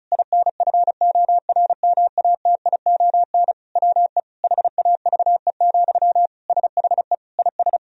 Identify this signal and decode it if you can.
Morse code, heard as a keyed tone, about 34 wpm, 700 Hz. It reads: INFORMATION WE HAVE, SHE IS
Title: Sentences from Top 1000 Words - Review Part 4 - 35wpm